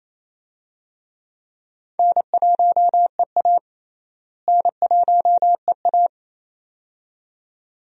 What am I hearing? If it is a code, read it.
Morse code, heard as a keyed tone, about 27 wpm, 700 Hz. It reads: N1EA N1EA
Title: USA Call Signs - 4 Characters - Repeated 2x - 28wpm